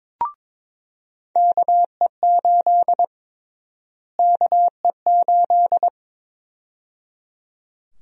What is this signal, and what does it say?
Morse code, heard as a keyed tone, about 22 wpm, 700 Hz. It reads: KE8 KE8
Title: USA Call Sign Prefixes - Repeated 2x - 22wpm